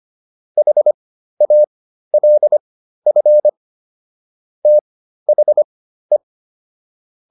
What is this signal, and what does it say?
Morse code, heard as a keyed tone, about 25 wpm, spaced as for 13 wpm, 600 Hz. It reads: HALF THE